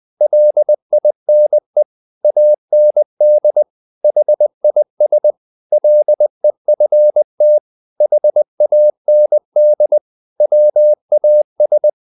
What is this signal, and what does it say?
Morse code, heard as a keyed tone, about 20 wpm, 600 Hz. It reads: LINE AND HIS LEFT HAND WAS
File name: the old man and the sea / the old man and the sea part 1 20wpm